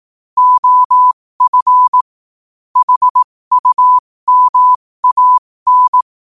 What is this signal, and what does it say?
Morse code, heard as a keyed tone, about 18 wpm, spaced as for 15 wpm, 1000 Hz. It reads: OF HUMAN
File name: qotdfiles / 15wpm